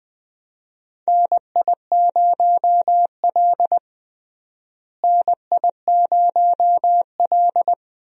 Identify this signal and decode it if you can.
Morse code, heard as a keyed tone, about 20 wpm, 700 Hz. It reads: NI0L NI0L
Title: USA Call Signs - 4 Characters - Repeated 2x - 20wpm